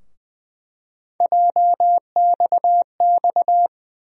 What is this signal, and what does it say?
Morse code, heard as a keyed tone, about 20 wpm, 700 Hz. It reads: JXX
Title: USA Call Sign Postfixes - 3 Letters 20wpm